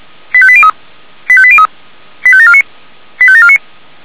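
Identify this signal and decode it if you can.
Five-tone selective call (ZVEI), heard as quick runs of five tones, tones of about 70 ms. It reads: 85803 85803 85749 85749
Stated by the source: ZVEI five-tone selective call received on SDR.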